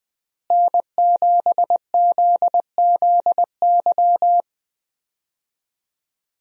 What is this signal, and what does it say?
Morse code, heard as a keyed tone, about 21 wpm, 700 Hz. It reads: N7ZZY
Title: USA Call Signs - 5 Characters - Repeated 3x - 20wpm